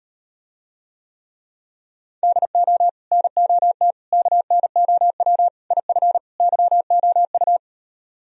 Morse code, heard as a keyed tone, about 38 wpm, 700 Hz. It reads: DO NOT KNOW IF YOU